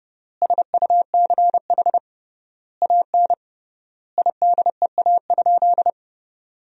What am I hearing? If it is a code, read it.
Morse code, heard as a keyed tone, about 30 wpm, 700 Hz. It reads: SUCH AN IDEA?
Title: 3x Word-Spacing - Sentences from Top 400 Words - 30wpm